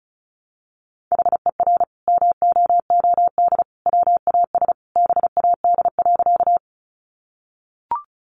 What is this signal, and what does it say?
Morse code, heard as a keyed tone, about 34 wpm, 700 Hz. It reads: HER MOOD WAS BAD.
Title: Sentences from Top 1100 Words 35wpm